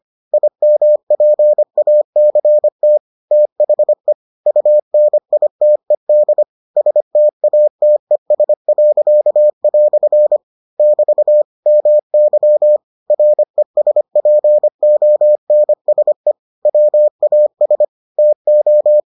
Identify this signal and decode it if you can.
Morse code, heard as a keyed tone, about 25 wpm, 600 Hz. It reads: IMPACT THE UNITED STATES." <BT> MY RESPONSE WAS TO